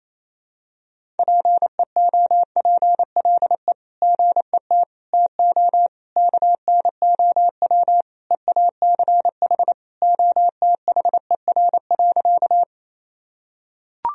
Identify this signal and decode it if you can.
Morse code, heard as a keyed tone, about 28 wpm, 700 Hz. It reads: PEOPLE GET TO KNOW EACH OTHER.
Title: Sentences from Top 1000 Words - Review Part 1 - 28wpm